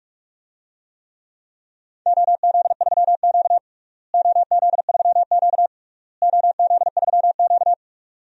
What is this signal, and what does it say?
Morse code, heard as a keyed tone, about 46 wpm, 700 Hz. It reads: OZ3Q OZ3Q OZ3Q
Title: DXCC-100 Call Signs - 4 Characters - Repeated 3x - 45wpm